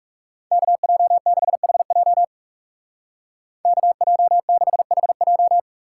Morse code, heard as a keyed tone, about 36 wpm, 700 Hz. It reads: KJ6HJ KJ6HJ
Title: Speed Racing - USA 2x2 Call Signs 30wpm